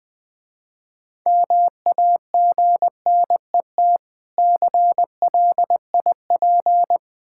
Morse code, heard as a keyed tone, about 22 wpm, 700 Hz. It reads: MAGNET CLIP
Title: Difficult Words - Speed-Racing with Hints - 20wpm